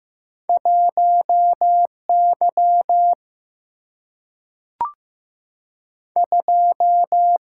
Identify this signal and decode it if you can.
Morse code, heard as a keyed tone, about 15 wpm, 700 Hz. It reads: 1 Y 2
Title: Single-Letter Single-Number 15wpm